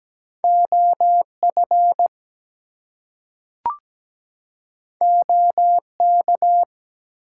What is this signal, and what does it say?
Morse code, heard as a keyed tone, about 17 wpm, 700 Hz. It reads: OF OK
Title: Two Letter Words - 17wpm